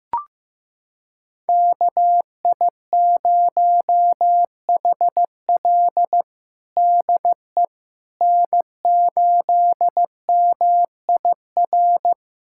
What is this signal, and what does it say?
Morse code, heard as a keyed tone, about 15 wpm, 700 Hz. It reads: KI0HL DE N8MIR
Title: USA Callsign DE Callsign - 5 Characters - 15wpm